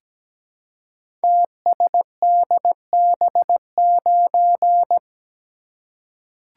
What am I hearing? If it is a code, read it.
Morse code, heard as a keyed tone, about 17 wpm, 700 Hz. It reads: TSDB9
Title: Code Groups - 5 Letter:Numbers - 17wpm